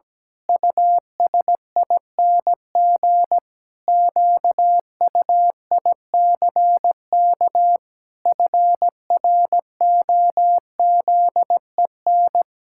Morse code, heard as a keyed tone, about 17 wpm, 700 Hz. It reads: USING QUICK FROZEN
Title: Pangram Challenge v2 - 17wpm